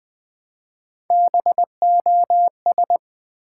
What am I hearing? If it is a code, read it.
Morse code, heard as a keyed tone, about 20 wpm, 700 Hz. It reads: BOS
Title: Build A Word - 1 Letter At A Time - 3 to 5-Letter Words - 20wpm